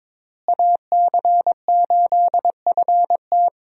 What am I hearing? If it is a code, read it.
Morse code, heard as a keyed tone, about 22 wpm, 700 Hz. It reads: AC8FT
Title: USA Call Signs - 5 Characters - Repeated 3x - 22wpm